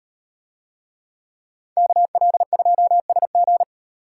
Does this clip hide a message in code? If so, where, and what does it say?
Yes: Morse code from 1.8 to 3.6 s, heard as a keyed tone, about 38 wpm, 700 Hz. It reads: KL2SG